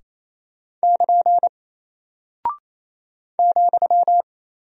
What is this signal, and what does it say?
Morse code, heard as a keyed tone, about 28 wpm, 700 Hz. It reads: <KN> ,
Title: Single Character (Letter:Number:Punctuation:Prosign) Alt 28wpm